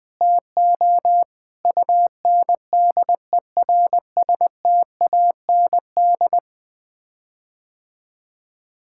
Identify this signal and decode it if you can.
Morse code, heard as a keyed tone, about 21 wpm, 700 Hz. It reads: TO UNDERSTAND
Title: Sets of 9 Words - 20wpm